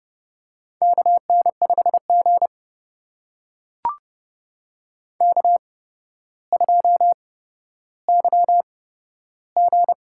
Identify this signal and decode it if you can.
Morse code, heard as a keyed tone, about 30 wpm, spaced as for 13 wpm, 700 Hz. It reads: KN5G K2YG
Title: USA Call Signs - 4 Characters - 8x Character Spacing - 30wpm